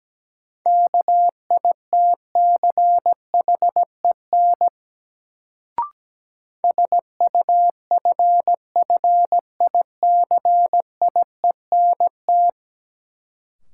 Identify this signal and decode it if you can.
Morse code, heard as a keyed tone, about 17 wpm, 700 Hz. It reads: KITCHEN SUFFICIENT
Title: Sentences from Top 1400 Words - New Word Drill - 17wpm